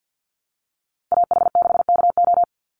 Morse code, heard as a keyed tone, about 49 wpm, 700 Hz. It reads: AH6KO
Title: Call Signs Repeated 3x - 50wpm